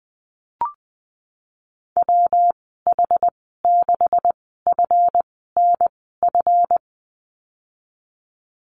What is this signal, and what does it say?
Morse code, heard as a keyed tone, about 20 wpm, spaced as for 15 wpm, 700 Hz. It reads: WH6FNF
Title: 154 Lesson 25g - Letter p - Call Signs 20wpm